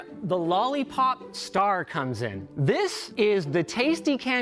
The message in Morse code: none